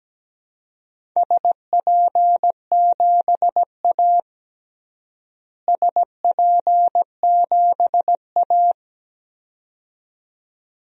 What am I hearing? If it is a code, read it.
Morse code, heard as a keyed tone, about 17 wpm, 700 Hz. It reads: SP7A SP7A
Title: DXCC-100 Call Signs - 4 Characters - Repeated 2x - 17wpm